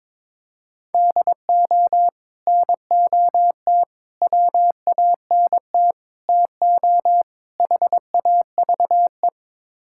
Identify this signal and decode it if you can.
Morse code, heard as a keyed tone, about 22 wpm, 700 Hz. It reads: DO NOT WANT TO HAVE
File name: practice / 22wpm / Build A Phrase - 5-Word Phrases - 22wpm